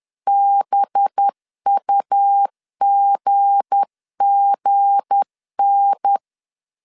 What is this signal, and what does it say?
Morse code, heard as a keyed tone, about 10 wpm, 800 Hz. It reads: BUGGN
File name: LEZIONI / 7 GF9Z / 7 G 10wpm 2501